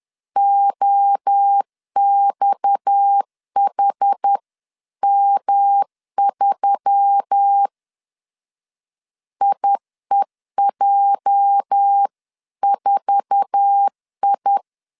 Morse code, heard as a keyed tone, about 10 wpm, 800 Hz. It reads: OXHM3 IEJ4I
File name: CW / DET / 1 9 1500C S A 10wpm 4511